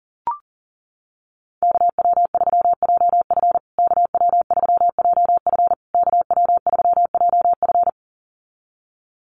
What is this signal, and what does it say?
Morse code, heard as a keyed tone, about 37 wpm, 700 Hz. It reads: KW3JF KW3JF KW3JF
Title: USA Call Signs Repeated 3x 40wpm